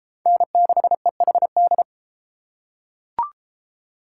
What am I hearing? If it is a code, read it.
Morse code, heard as a keyed tone, about 33 wpm, 700 Hz. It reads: N6EHD